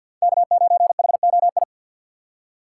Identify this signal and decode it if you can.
Morse code, heard as a keyed tone, about 54 wpm, 700 Hz. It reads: K9HOI
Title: USA Call Signs - 5 Characters - 8x Character Spacing - 50wpm